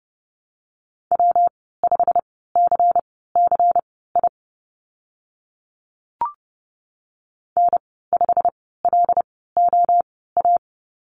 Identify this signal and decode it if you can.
Morse code, heard as a keyed tone, about 30 wpm, spaced as for 17 wpm, 700 Hz. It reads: W5CCI N5LOA
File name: Lesson 19g - Letter w - Call Signs 30wpm